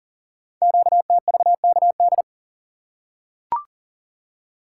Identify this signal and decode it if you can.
Morse code, heard as a keyed tone, about 40 wpm, 700 Hz. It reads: QTVKD
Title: Code Groups - 5 Letter:Numbers - 40wpm